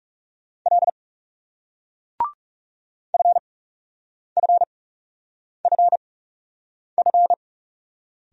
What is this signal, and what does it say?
Morse code, heard as a keyed tone, about 42 wpm, spaced as for 8 wpm, 700 Hz. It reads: L FFFF